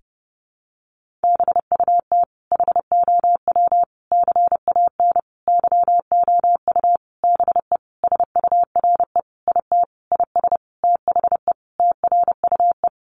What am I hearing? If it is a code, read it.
Morse code, heard as a keyed tone, about 30 wpm, 700 Hz. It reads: BUT HOW CAN YOU BE SURE IT IS THE TRUE
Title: Sentences from Top 1400 Words 30wpm